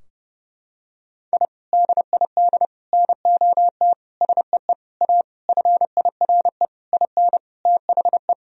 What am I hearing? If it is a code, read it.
Morse code, heard as a keyed tone, about 32 wpm, 700 Hz. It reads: I DID NOT SEE A FIRE IN THE
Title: Sentences from Top 1000 Words - Review Part 2 - 30wpm